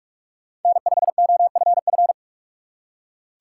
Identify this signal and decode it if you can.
Morse code, heard as a keyed tone, about 46 wpm, 700 Hz. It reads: N5OFF
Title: USA Call Signs - 6x Character Spacing - 45wpm